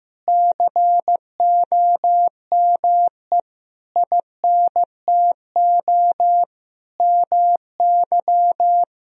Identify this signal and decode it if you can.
Morse code, heard as a keyed tone, about 15 wpm, 700 Hz. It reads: COME INTO MY
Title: Sentences from Top 1000 Words - Review Part 1 - 15wpm